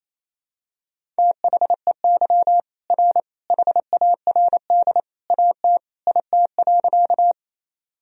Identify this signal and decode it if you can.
Morse code, heard as a keyed tone, about 28 wpm, 700 Hz. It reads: THEY R HARD AT IT.
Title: Sentences from Top 300 Words and Abbreviations 28wpm